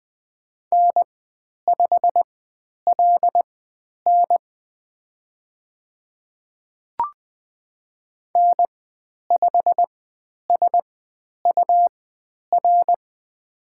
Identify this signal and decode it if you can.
Morse code, heard as a keyed tone, about 20 wpm, spaced as for 10 wpm, 700 Hz. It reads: N5LN N5SUR
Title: 109 Lesson 18g - Letter m - Call Signs 20wpm